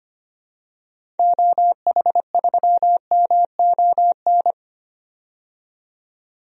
Morse code, heard as a keyed tone, about 25 wpm, 700 Hz. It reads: OH3MON
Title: DXCC-100 Call Signs - 6 Characters - 25wpm